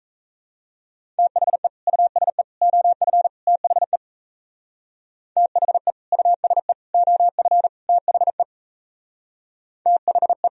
Morse code, heard as a keyed tone, about 37 wpm, 700 Hz. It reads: THE USE OF THE THE USE OF THE THE